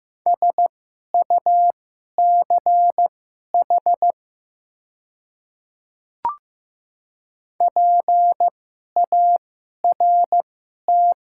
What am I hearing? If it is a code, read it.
Morse code, heard as a keyed tone, about 15 wpm, spaced as for 11 wpm, 700 Hz. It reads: SUCH PART
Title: 2x Character Spacing - Top 300 Words - 15wpm